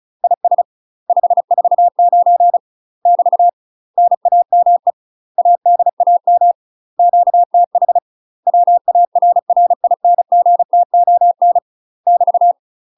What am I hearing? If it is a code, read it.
Morse code, heard as a keyed tone, about 34 wpm, 700 Hz. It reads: IS 549 <BT> NAME ADAM QTH WARRINGTON <BT>
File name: UK MASTER QSOs TRAINING / 35WPM / UK MASTER QSOs 0037